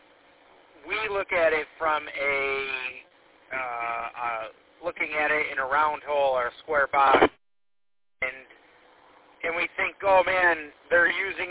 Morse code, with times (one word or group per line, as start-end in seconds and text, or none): none